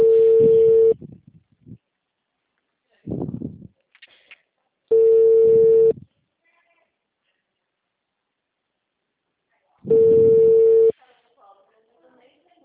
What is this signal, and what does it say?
Ringback tone (China, 450 Hz), with speech.